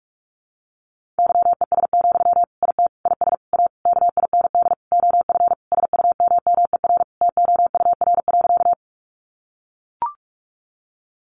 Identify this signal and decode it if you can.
Morse code, heard as a keyed tone, about 44 wpm, 700 Hz. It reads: YES, IT IS A KIND OF SUMMER TOUR.
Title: Sentences from Top 1400 Words 45wpm